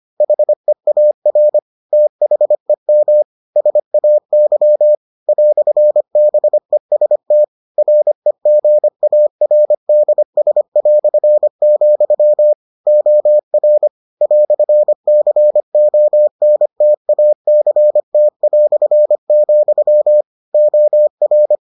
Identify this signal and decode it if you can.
Morse code, heard as a keyed tone, about 25 wpm, 600 Hz. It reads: HEAR THEM SAY "BEST REGARDS", OR "CONTACT", OR